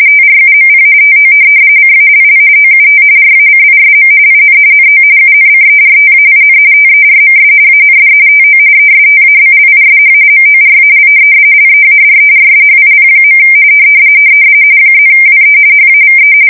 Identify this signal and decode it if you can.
RTTY, about 45.45 baud. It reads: TRUMP TO CALL OUT RACISM IN NO UNCERTAIN TERMS AND TO ESTABLISH MORAL LEADERSHIP. +WE HAVE
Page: 170 Shift 2125/2295